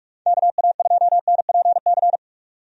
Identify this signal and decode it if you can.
Morse code, heard as a keyed tone, about 44 wpm, 700 Hz. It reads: KA2NPC
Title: USA Call Signs - 6 Characters - 45wpm